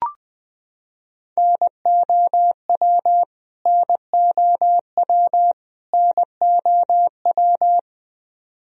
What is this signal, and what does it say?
Morse code, heard as a keyed tone, about 20 wpm, 700 Hz. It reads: NOW NOW NOW
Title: Top 100 Words Repeated 20wpm